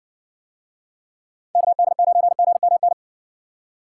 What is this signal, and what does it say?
Morse code, heard as a keyed tone, about 60 wpm, 700 Hz. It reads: KD9GMN